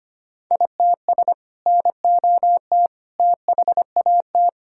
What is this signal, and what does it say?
Morse code, heard as a keyed tone, about 25 wpm, 700 Hz. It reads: ITS NOT THAT
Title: Sentences from Top 1000 Words - Review Part 5 - 25wpm